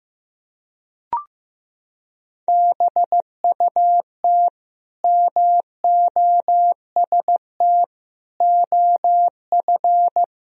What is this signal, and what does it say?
Morse code, heard as a keyed tone, about 15 wpm, 700 Hz. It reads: BUT MOST OF
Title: Sentences from Top 1000 Words - Review Part 3 - 15wpm